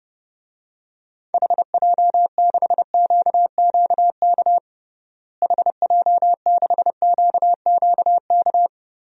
Morse code, heard as a keyed tone, about 30 wpm, 700 Hz. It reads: HJ6QQK HJ6QQK
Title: DXCC-100 Call Signs - 6 Characters - Repeated 3x - 30wpm